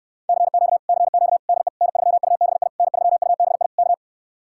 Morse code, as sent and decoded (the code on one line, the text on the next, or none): -.-. --.- / -.-. --.- / -.. / -. ...-- ..- -... .. / -. ...-- ..- -... .. / -.-
CQ CQ DE N3UBI N3UBI K